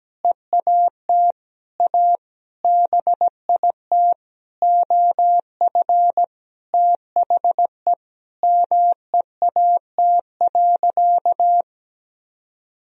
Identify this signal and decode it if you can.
Morse code, heard as a keyed tone, about 17 wpm, 700 Hz. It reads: EAT A BIT OF THE MEAT.
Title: Sentences from Top 1000 Words - Review Part 3 - 17wpm